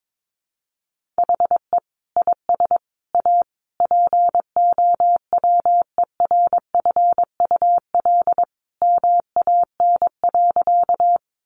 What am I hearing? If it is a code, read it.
Morse code, heard as a keyed tone, about 22 wpm, 700 Hz. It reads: HE IS A POWERFUL MAN.
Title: Sentences from Top 1300 Words 22wpm